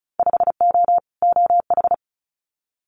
Morse code, heard as a keyed tone, about 34 wpm, 700 Hz. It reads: 5O OH
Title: Winter Field Day Exchanges Encore 35wpm